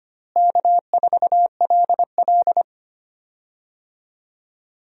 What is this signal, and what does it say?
Morse code, heard as a keyed tone, about 25 wpm, 700 Hz. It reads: K4LL
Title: USA Call Signs - 4 Characters - Repeated 3x - 25wpm